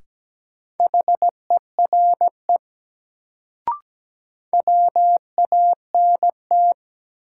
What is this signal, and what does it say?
Morse code, heard as a keyed tone, about 17 wpm, 700 Hz. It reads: HERE WANT
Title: Sentences from Top 200 Words - New Word Drill - 17wpm